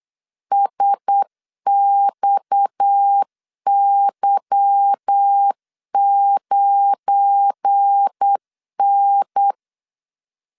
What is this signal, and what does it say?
Morse code, heard as a keyed tone, about 8 wpm, 800 Hz. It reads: SXY9N
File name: LEZIONI / 8 YJ45 / 8 Y 8wpm 3245